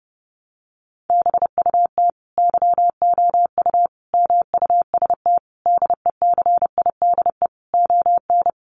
Morse code, heard as a keyed tone, about 30 wpm, 700 Hz. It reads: BUT YOU MUST DECIDE ON
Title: Sentences from Top 600 Words 30wpm